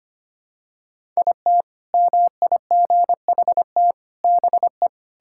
Morse code, heard as a keyed tone, about 25 wpm, 700 Hz. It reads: IT MIGHT BE
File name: practice / 25wpm / Sentences with 2-Word Phrases - 25wpm